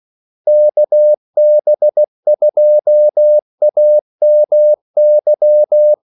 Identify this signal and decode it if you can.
Morse code, heard as a keyed tone, about 16 wpm, 600 Hz. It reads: KB2AMY
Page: Kaldesignaler 16 wpm